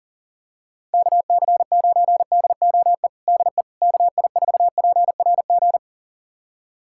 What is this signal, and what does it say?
Morse code, heard as a keyed tone, about 37 wpm, 700 Hz. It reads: KC9DOE DE KI4PRG
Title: USA Callsign DE Callsign - 6 Characters - 40wpm